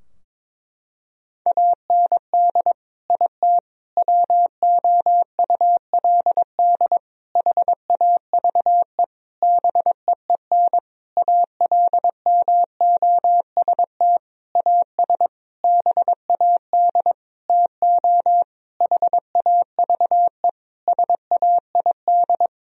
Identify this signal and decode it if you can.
Morse code, heard as a keyed tone, about 22 wpm, 700 Hz. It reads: AND IT WOULD HAVE BEEN ALMOST AS BAD TO HAVE SAID
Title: Sentences from Top 1000 Words - Review Part 4 - 22wpm